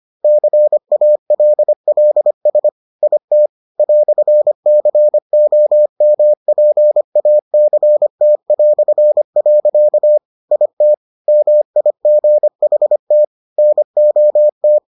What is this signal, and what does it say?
Morse code, heard as a keyed tone, about 25 wpm, 600 Hz. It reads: CALLS IT "COMPACT". IT MIGHT NOT